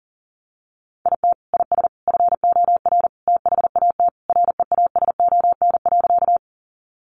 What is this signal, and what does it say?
Morse code, heard as a keyed tone, about 40 wpm, 700 Hz. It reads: IT IS FOR THAT REASON.
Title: Sentences from Top 700 Words 40wpm